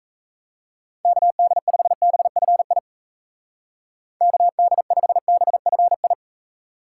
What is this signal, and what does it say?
Morse code, heard as a keyed tone, about 36 wpm, 700 Hz. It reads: KD5BFI KD5BFI